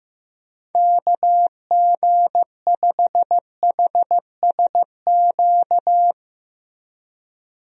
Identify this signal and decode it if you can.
Morse code, heard as a keyed tone, about 15 wpm, 700 Hz. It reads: KG5HSQ
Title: USA Call Signs - 6 Characters - 15wpm